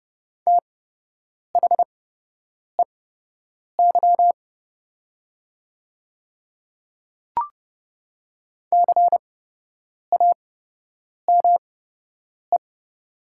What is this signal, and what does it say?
Morse code, heard as a keyed tone, about 30 wpm, spaced as for 8 wpm, 700 Hz. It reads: THEY CAME
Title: 8x Character Spacing - Top 200 Words - 30wpm